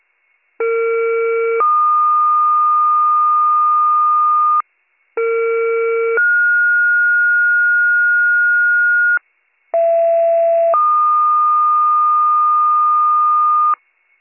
Two-tone page, heard as a one-second tone, then a three-second tone, about 460 then 1220 Hz, about 460 then 1430 Hz, about 670 then 1220 Hz. It is heard from 0.6 to 13.7 s.